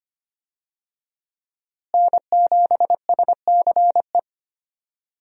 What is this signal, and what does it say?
Morse code, heard as a keyed tone, about 25 wpm, 700 Hz. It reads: N7SCE
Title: USA Call Signs - 5 Characters - Repeated 3x - 25wpm